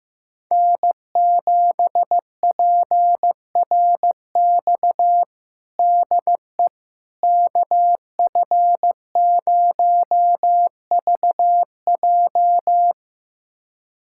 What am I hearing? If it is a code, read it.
Morse code, heard as a keyed tone, about 16 wpm, 700 Hz. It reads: N7PRX DE KF0VJ
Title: USA Callsign DE Callsign - 5 Characters - 15wpm